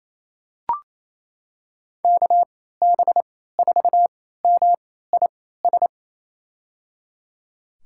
Morse code, heard as a keyed tone, about 28 wpm, spaced as for 16 wpm, 700 Hz. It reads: KB4MIS
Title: USA Call Signs - 6 Characters - 3x Character Spacing - 28wpm